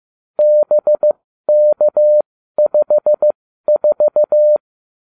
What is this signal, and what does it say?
Morse code, heard as a keyed tone, about 15 wpm, spaced as for 12 wpm, 600 Hz. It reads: BK54